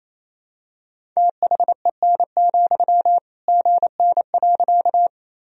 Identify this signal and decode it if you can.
Morse code, heard as a keyed tone, about 28 wpm, 700 Hz. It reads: THEN, GN.
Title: Sentences from Top 300 Words and Abbreviations 28wpm